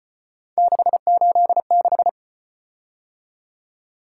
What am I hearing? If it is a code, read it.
Morse code, heard as a keyed tone, about 34 wpm, 700 Hz. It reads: 686